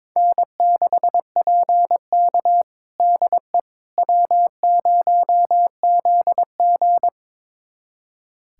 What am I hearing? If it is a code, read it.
Morse code, heard as a keyed tone, about 22 wpm, 700 Hz. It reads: N6PK DE W0ZG
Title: USA Callsign DE Callsign - 4 Characters - 22wpm